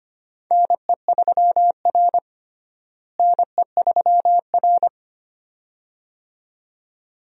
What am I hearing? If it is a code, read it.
Morse code, heard as a keyed tone, about 25 wpm, 700 Hz. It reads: NE3R NE3R
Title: USA Call Signs - 4 Characters - Repeated 2x - 25wpm